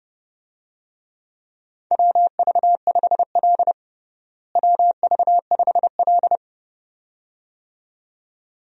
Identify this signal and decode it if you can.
Morse code, heard as a keyed tone, about 31 wpm, 700 Hz. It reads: WV5L WV5L
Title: USA Call Signs - 4 Characters - Repeated 2x - 30wpm